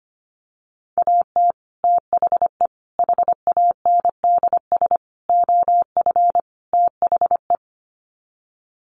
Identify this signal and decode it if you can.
Morse code, heard as a keyed tone, about 25 wpm, 700 Hz. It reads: AT THE HANDS OF THE
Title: Sets of 5 Words Encore2 25wpm